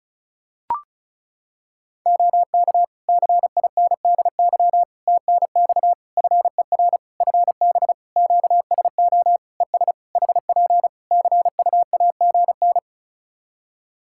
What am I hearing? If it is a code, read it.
Morse code, heard as a keyed tone, about 36 wpm, 700 Hz. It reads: OK CINDY TNX FER FB QSO ES HP CUAGN
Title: K1USN SST QSO Practice 35wpm